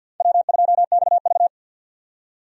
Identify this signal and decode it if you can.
Morse code, heard as a keyed tone, about 46 wpm, 700 Hz. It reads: W2XV
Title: USA Call Signs - 4 Characters - 50wpm